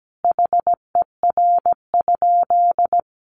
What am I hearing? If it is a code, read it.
Morse code, heard as a keyed tone, about 17 wpm, 700 Hz. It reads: HER?
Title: Sentences from Top 1000 Words 17wpm